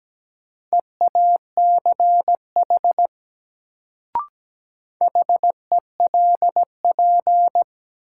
Morse code, heard as a keyed tone, about 17 wpm, 700 Hz. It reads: EACH HELP
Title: Sentences from Top 200 Words - New Word Drill - 17wpm